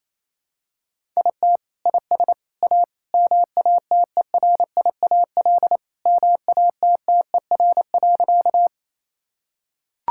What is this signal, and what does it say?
Morse code, heard as a keyed tone, about 28 wpm, 700 Hz. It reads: IT IS A MATERIAL MATTER.
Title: Sentences from Top 1000 Words - Review Part 5 - 28wpm